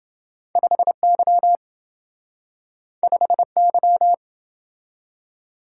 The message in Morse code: 5Y 5Y

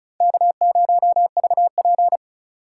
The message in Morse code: K0VP

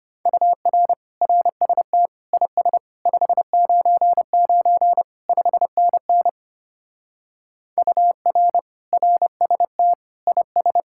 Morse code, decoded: UR RST IS 599 5NN UR RST IS

